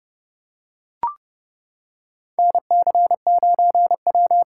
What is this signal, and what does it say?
Morse code, heard as a keyed tone, about 30 wpm, 700 Hz. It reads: NC9W
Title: K1USN SST QSO Practice - 30wpm